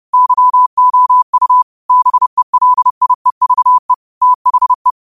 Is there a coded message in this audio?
Yes: Morse code from 0.1 to 4.9 s, heard as a keyed tone, about 30 wpm, 1000 Hz. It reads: YOU BELIEVE THE